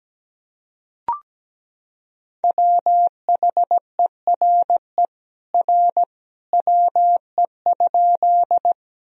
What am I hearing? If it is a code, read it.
Morse code, heard as a keyed tone, about 17 wpm, 700 Hz. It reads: WHERE R WE?
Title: Sentences from Top 200 Words and Abbreviations 17wpm